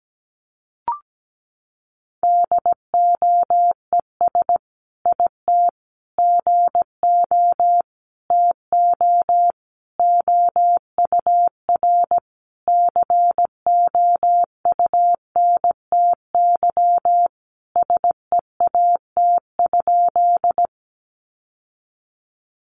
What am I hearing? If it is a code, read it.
Morse code, heard as a keyed tone, about 17 wpm, 700 Hz. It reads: DOES IT GO TO OUR COUNTY SEAT?
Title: Sentences from Top 1500 Words 17wpm